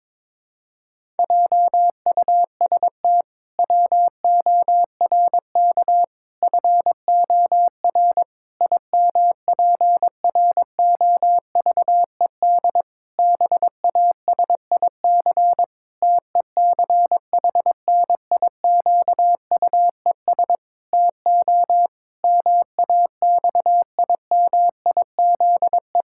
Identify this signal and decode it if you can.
Morse code, heard as a keyed tone, about 22 wpm, 700 Hz. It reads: JUST WORK FOR IMPROVED BASIC TECHNIQUES TO MAXIMIZE
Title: Pangram Challenge v2 - 22wpm